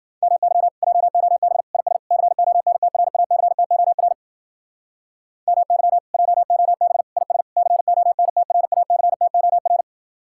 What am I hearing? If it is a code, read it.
Morse code, heard as a keyed tone, about 48 wpm, 700 Hz. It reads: MY JOB IS CONTRACTOR MY JOB IS CONTRACTOR
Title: Speed-Racing Common QSO Phrases - 40wpm